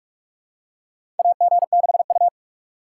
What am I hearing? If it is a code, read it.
Morse code, heard as a keyed tone, about 45 wpm, 700 Hz. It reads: AG6U
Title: USA Call Signs - 4 Characters - 2x Character Spacing - 45wpm